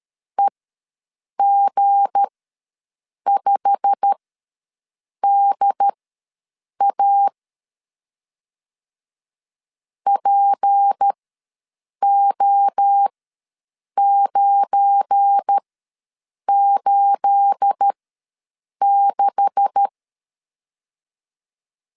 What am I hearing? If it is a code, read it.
Morse code, heard as a keyed tone, about 13 wpm, spaced as for 7 wpm, 800 Hz. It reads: EG5DA PO986
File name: LEZIONI / 10 SIGNS / 10A SLASH ws 12wpm 3443